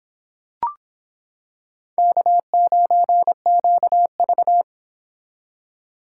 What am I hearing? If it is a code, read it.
Morse code, heard as a keyed tone, about 26 wpm, 700 Hz. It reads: K9QV